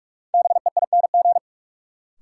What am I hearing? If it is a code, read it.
Morse code, heard as a keyed tone, about 46 wpm, 700 Hz. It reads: BEING
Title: Sets of 4 Words - New Word Drill - 45wpm